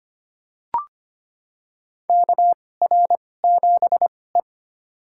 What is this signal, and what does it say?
Morse code, heard as a keyed tone, about 25 wpm, spaced as for 18 wpm, 700 Hz. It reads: KR7E
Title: USA Call Signs - 2x Character Spacing - 25wpm